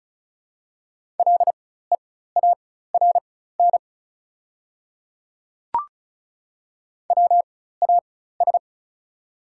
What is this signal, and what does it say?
Morse code, heard as a keyed tone, about 34 wpm, spaced as for 16 wpm, 700 Hz. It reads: LEARN WAS
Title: 4x Character Spacing - Top 300 Words - 35wpm